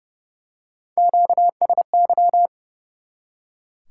Morse code, heard as a keyed tone, about 30 wpm, 700 Hz. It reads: QSY
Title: Top 180 QSO Elements 30wpm